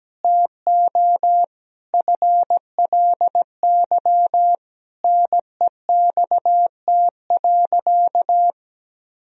Morse code, heard as a keyed tone, about 17 wpm, 700 Hz. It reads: TO FLY NEXT.
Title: Sentences from Top 1000 Words - Review Part 2 - 17wpm